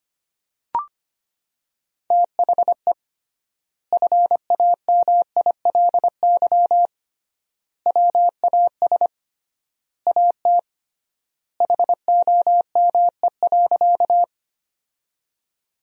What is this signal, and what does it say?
Morse code, heard as a keyed tone, about 25 wpm, 700 Hz. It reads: THE FAMILY WAS AT HOME.
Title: 3x Word-Spacing - Sentences from Top 400 Words - 25wpm